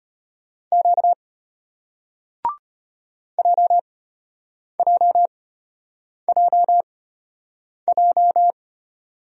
Morse code, heard as a keyed tone, about 33 wpm, spaced as for 8 wpm, 700 Hz. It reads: Q JJJJ